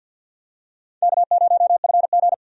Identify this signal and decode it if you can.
Morse code, heard as a keyed tone, about 53 wpm, 700 Hz. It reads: K0FG
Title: USA Call Signs - 4 Characters - 50wpm